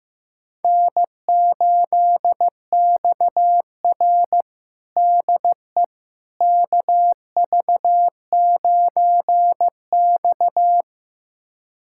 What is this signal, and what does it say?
Morse code, heard as a keyed tone, about 16 wpm, 700 Hz. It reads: N8XR DE KV9X
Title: USA Callsign DE Callsign - 4 Characters - 15wpm